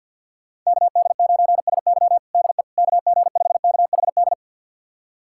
Morse code, heard as a keyed tone, about 51 wpm, 700 Hz. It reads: KD9SY DE KG5KHD
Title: USA Callsign DE Callsign 50wpm